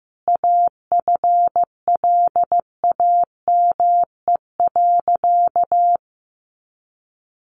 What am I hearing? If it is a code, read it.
Morse code, heard as a keyed tone, about 15 wpm, 700 Hz. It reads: AFLAME.
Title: A Sound Of Thunder 15wpm